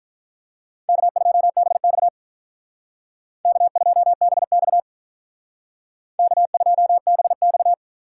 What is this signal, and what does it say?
Morse code, heard as a keyed tone, about 47 wpm, 700 Hz. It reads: K2BX K2BX K2BX